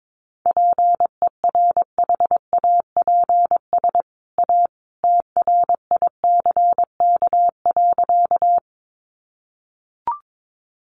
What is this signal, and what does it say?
Morse code, heard as a keyed tone, about 22 wpm, 700 Hz. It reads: PERHAPS A TRICK.
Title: Sentences from Top 1500 Words 22wpm